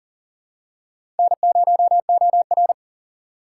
Morse code, heard as a keyed tone, about 38 wpm, 700 Hz. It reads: N0OR
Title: USA Call Signs - 4 Characters - 3x Character Spacing - 40wpm